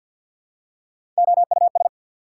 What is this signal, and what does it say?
Morse code, heard as a keyed tone, about 54 wpm, 700 Hz. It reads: OUS
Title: Prefixes and Suffixes 50wpm